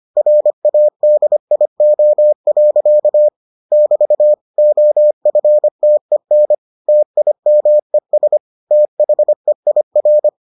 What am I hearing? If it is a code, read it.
Morse code, heard as a keyed tone, about 25 wpm, 600 Hz. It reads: RADIO. <BT> OFTEN TIMES THEIR